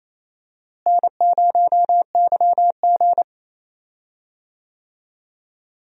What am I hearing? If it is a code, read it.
Morse code, heard as a keyed tone, about 28 wpm, 700 Hz. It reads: N0YG
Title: USA Call Signs - 4 Characters - Repeated 2x - 28wpm